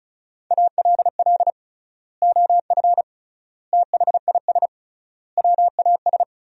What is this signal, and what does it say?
Morse code, heard as a keyed tone, about 36 wpm, 700 Hz. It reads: ALL OF THIS WAS
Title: 3x Word-Spacing - Sentences from Top 500 Words - 35wpm